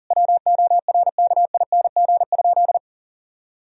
Morse code, heard as a keyed tone, about 40 wpm, 700 Hz. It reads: WORKING?
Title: Sentences from Top 1000 Words - Review Part 4 - 40wpm